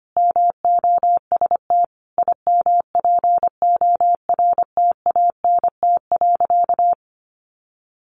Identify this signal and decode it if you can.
Morse code, heard as a keyed tone, about 25 wpm, 700 Hz. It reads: MOST IMPORTANT.
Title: Sentences from Top 700 Words 25wpm